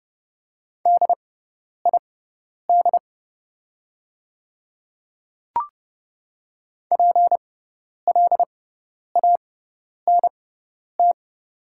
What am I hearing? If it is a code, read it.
Morse code, heard as a keyed tone, about 30 wpm, spaced as for 10 wpm, 700 Hz. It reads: DID PLANT
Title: 6x Character Spacing - Top 300 Words - 30wpm